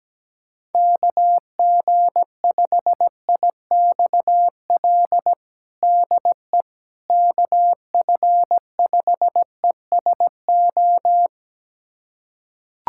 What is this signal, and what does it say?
Morse code, heard as a keyed tone, about 17 wpm, 700 Hz. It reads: KG5IXL DE KF5ESO
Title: USA Callsign DE Callsign - 6 Characters - 17wpm